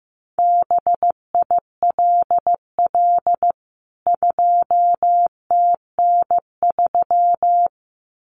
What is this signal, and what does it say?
Morse code, heard as a keyed tone, about 15 wpm, 700 Hz. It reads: BILL 2TN3
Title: CWT Exchange 15wpm